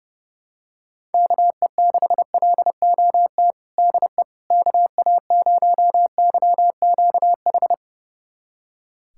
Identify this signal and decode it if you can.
Morse code, heard as a keyed tone, about 32 wpm, 700 Hz. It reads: KE6LOT DE KA0YQH
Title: USA Callsign DE Callsign 30wpm